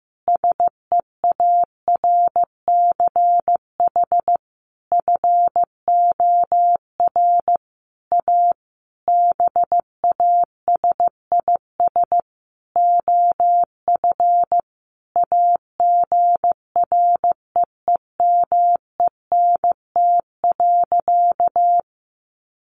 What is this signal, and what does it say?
Morse code, heard as a keyed tone, about 15 wpm, 700 Hz. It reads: SEARCH FOR A BASIS OF AGREEMENT.
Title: Sentences from Top 1500 Words 15wpm